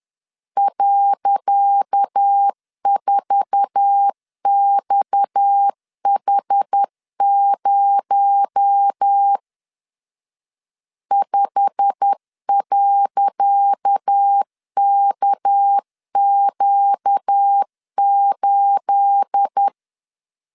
Morse code, heard as a keyed tone, about 10 wpm, 800 Hz. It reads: .4XH0 5.KQ8
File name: LEZIONI / 10 SIGNS / 10C DOT 10wpm 2940